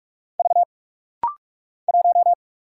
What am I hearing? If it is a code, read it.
Morse code, heard as a keyed tone, about 46 wpm, 700 Hz. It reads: V 1
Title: Single Letter-Number Alt - Rapid-Fire 45wpm